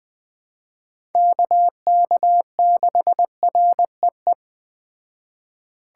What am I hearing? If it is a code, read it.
Morse code, heard as a keyed tone, about 21 wpm, 700 Hz. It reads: KK6REE